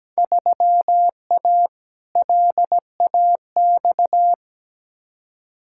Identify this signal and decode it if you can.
Morse code, heard as a keyed tone, about 17 wpm, 700 Hz. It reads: 3A LAX
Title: Field Day Exchanges - 17wpm